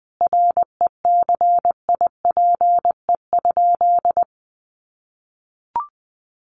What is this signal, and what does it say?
Morse code, heard as a keyed tone, about 20 wpm, 700 Hz. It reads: RECIPE?
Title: Sentences from Top 1200 Words 20wpm